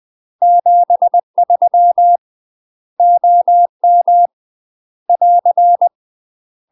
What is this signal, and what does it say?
Morse code, heard as a keyed tone, about 20 wpm, 700 Hz. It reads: 73 OM <AR>
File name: UK MASTER QSOs TRAINING / 20WPM / UK MASTER QSOs 0044